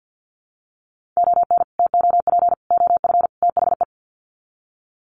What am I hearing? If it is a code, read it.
Morse code, heard as a keyed tone, about 50 wpm, 700 Hz. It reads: ON TOP OF THE
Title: Sets of 4 Words 50wpm